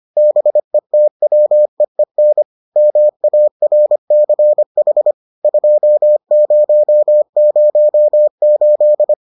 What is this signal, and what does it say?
Morse code, heard as a keyed tone, about 25 wpm, 600 Hz. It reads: BETWEEN MARCH 2008